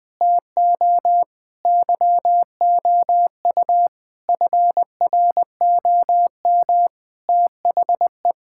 Morse code, heard as a keyed tone, about 20 wpm, 700 Hz. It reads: TO YOU FROM THE
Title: Sentences from Top 1000 Words - Review Part 5 - 20wpm